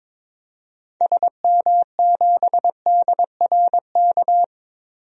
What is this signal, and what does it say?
Morse code, heard as a keyed tone, about 22 wpm, 700 Hz. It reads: SM7DRK
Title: DXCC-100 Call Signs - 6 Characters - Repeated 2x - 22wpm